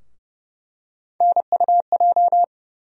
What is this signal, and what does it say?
Morse code, heard as a keyed tone, about 30 wpm, 700 Hz. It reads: NUJ
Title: USA Call Sign Postfixes - 3 Letters 30wpm